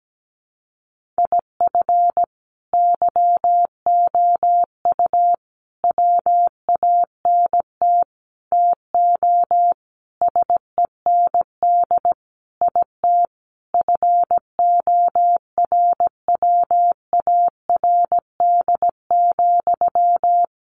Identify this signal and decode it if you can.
Morse code, heard as a keyed tone, about 17 wpm, 700 Hz. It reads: IF YOU WANT TO SEND IT FORWARD,
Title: Sentences from Top 1500 Words 17wpm